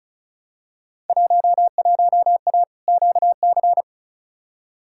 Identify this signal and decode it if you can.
Morse code, heard as a keyed tone, about 36 wpm, 700 Hz. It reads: 11A QC
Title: Field Day Exchanges Encore - 35wpm